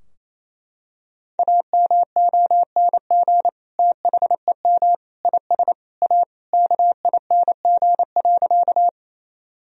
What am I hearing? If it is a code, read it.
Morse code, heard as a keyed tone, about 28 wpm, 700 Hz. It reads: AMONG THEM IS A KING.
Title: Sentences from Top 1000 Words - Review Part 3 - 28wpm